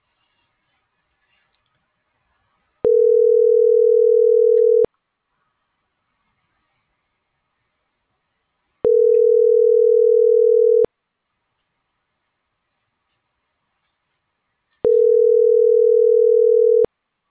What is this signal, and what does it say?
Ringback tone (North America), 440 + 480 Hz.